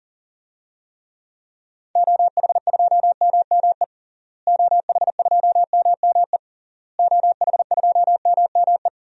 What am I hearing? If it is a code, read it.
Morse code, heard as a keyed tone, about 40 wpm, 700 Hz. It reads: OH2MME OH2MME OH2MME
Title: DXCC-100 Call Signs - 6 Characters - Repeated 3x - 40wpm